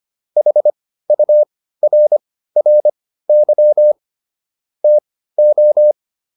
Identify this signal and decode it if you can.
Morse code, heard as a keyed tone, about 25 wpm, spaced as for 15 wpm, 600 Hz. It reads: HURRY TO